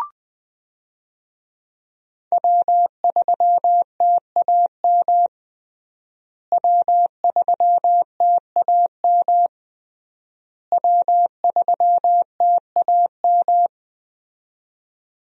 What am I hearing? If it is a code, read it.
Morse code, heard as a keyed tone, about 20 wpm, 700 Hz. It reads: W3TAM W3TAM W3TAM
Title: USA Call Signs - 5 Characters - Repeated 3x - 20wpm